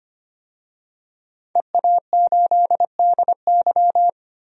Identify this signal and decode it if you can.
Morse code, heard as a keyed tone, about 25 wpm, 700 Hz. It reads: EA8DY
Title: DXCC-100 Call Signs - 5 Characters - 25wpm